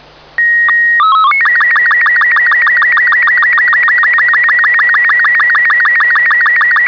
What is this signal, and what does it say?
SSTV (slow-scan television), Robot 36.